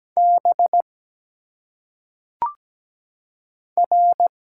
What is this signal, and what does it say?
Morse code, heard as a keyed tone, about 17 wpm, 700 Hz. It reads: B R
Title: Single Letters - Phonetic Pronunciation 17wpm